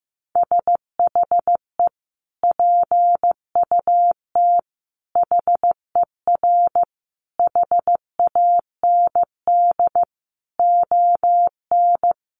Morse code, heard as a keyed tone, about 15 wpm, 700 Hz. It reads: SHE PUT HER HAND ON
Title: Sentences from Top 1500 Words 15wpm